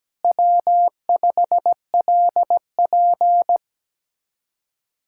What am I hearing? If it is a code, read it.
Morse code, heard as a keyed tone, about 17 wpm, 700 Hz. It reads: W5LP
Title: Speed-Racing Common QSO Phrases - 17wpm